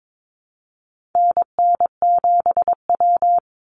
Morse code, heard as a keyed tone, about 22 wpm, 700 Hz. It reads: NN7W
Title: Call Signs - 22wpm